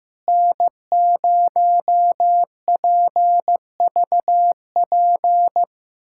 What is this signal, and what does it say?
Morse code, heard as a keyed tone, about 15 wpm, 700 Hz. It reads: N0PVP
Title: USA Callsign DE Callsign - 5 Characters - 15wpm